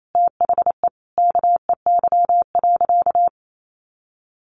Morse code, heard as a keyed tone, about 28 wpm, 700 Hz. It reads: THE KEY.
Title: Sentences from Top 1300 Words 28wpm